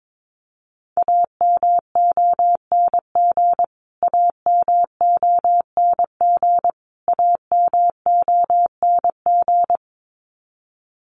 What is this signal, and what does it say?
Morse code, heard as a keyed tone, about 22 wpm, 700 Hz. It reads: AMONG AMONG AMONG
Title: Top 500 Words Repeated 22wpm